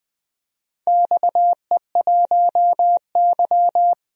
Morse code, heard as a keyed tone, about 21 wpm, 700 Hz. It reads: XE1Y
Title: DXCC-100 Call Signs - 4 Characters - 20wpm